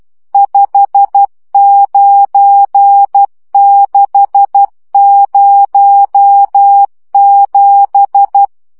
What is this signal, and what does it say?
Morse code, heard as a keyed tone, about 12 wpm, 800 Hz. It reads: 59607